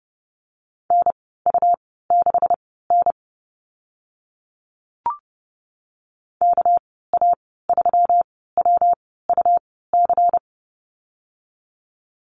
Morse code, heard as a keyed tone, about 30 wpm, spaced as for 17 wpm, 700 Hz. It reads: NU6N KA3WUC
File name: Lesson 37g - Letter x - Call Signs 30wpm